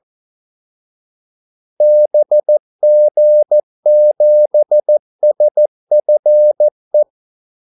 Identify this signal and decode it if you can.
Morse code, heard as a keyed tone, about 14 wpm, 600 Hz. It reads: BG7SFE
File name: Callsigns-14wpm